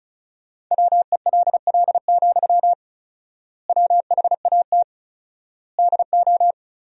Morse code, heard as a keyed tone, about 36 wpm, 700 Hz. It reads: WELL, WHAT DO
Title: 4x Word-Spacing - Sentences from Top 200 Words - 35wpm